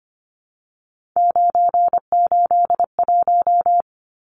Morse code, heard as a keyed tone, about 25 wpm, 700 Hz. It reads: 981
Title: Numbers - 3-Digit 25wpm